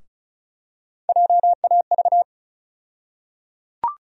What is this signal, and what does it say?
Morse code, heard as a keyed tone, about 34 wpm, 700 Hz. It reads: JAV
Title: USA Call Sign Postfixes - 3 Letters 35wpm